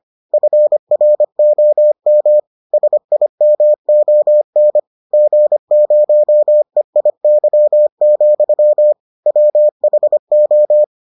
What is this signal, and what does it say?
Morse code, heard as a keyed tone, about 25 wpm, 600 Hz. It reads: FROM SIMON G0EIY, WHO